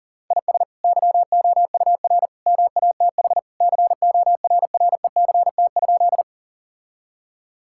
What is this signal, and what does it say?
Morse code, heard as a keyed tone, about 40 wpm, 700 Hz. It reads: IS YOUR MATH CORRECT?
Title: Sentences from Top 1000 Words - Review Part 6 - 40wpm